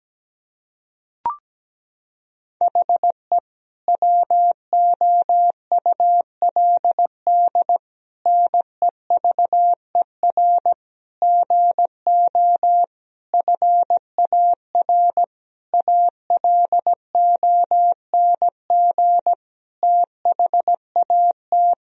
Morse code, heard as a keyed tone, about 17 wpm, 700 Hz. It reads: HE WOULD NEVER GO FAR ALONG THAT
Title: Sentences from Top 1000 Words - Review Part 2 - 17wpm